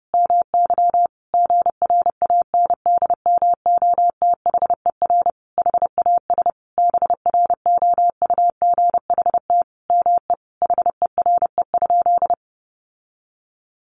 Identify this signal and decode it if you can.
Morse code, heard as a keyed tone, about 30 wpm, 700 Hz. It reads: MY GRANDMOTHER HAS BROUGHT ME HERE?
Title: Sentences from Top 1200 Words 30wpm